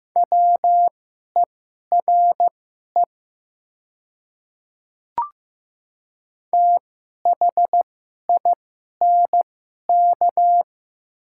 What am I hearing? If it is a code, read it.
Morse code, heard as a keyed tone, about 15 wpm, spaced as for 11 wpm, 700 Hz. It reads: WERE THINK
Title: 2x Character Spacing - Top 200 Words - 15wpm